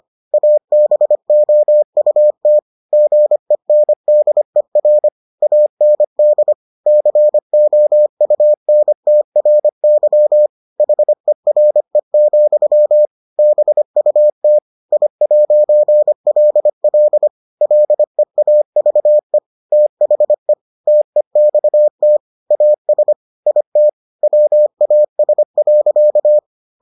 Morse code, heard as a keyed tone, about 25 wpm, 600 Hz. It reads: ABOUT GENDER AND COUNTRY HERE, BUT I'LL LEAVE THE TEXT AS IT WAS.